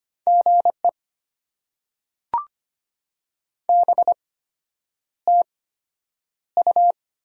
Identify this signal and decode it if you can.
Morse code, heard as a keyed tone, about 25 wpm, spaced as for 9 wpm, 700 Hz. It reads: GE BTU
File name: Top 90 QSO Elements - 8x Character Spacing - 25wpm